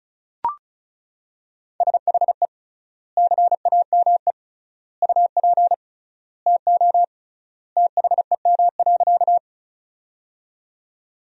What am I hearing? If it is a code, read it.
Morse code, heard as a keyed tone, about 34 wpm, 700 Hz. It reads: SHE CAME UP TO THEM.
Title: 3x Word-Spacing - Sentences from Top 200 Words - 35wpm